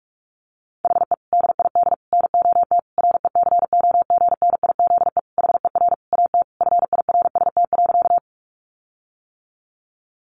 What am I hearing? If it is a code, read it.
Morse code, heard as a keyed tone, about 44 wpm, 700 Hz. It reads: HE DID NOT RECOGNIZE HER AT FIRST.
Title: Sentences from Top 1400 Words 45wpm